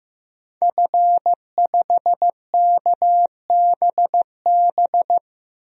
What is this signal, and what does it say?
Morse code, heard as a keyed tone, about 15 wpm, 700 Hz. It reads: F5KBB
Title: DXCC-100 Call Signs - 5 Characters - Repeated 3x - 15wpm